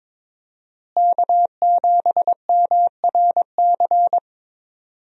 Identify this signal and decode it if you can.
Morse code, heard as a keyed tone, about 21 wpm, 700 Hz. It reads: K7MRC
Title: USA Call Signs - 5 Characters - Repeated 2x - 22wpm